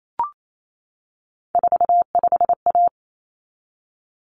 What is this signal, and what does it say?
Morse code, heard as a keyed tone, about 28 wpm, 700 Hz. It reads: 45A
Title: Common Cut Numbers - 3-Digit 28wpm